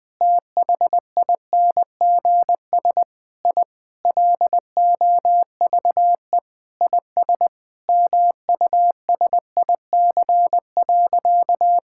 Morse code, heard as a keyed tone, about 20 wpm, 700 Hz. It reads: THINGS I LOVE IS MUSIC.
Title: Sentences with 4-Word Phrases - 20wpm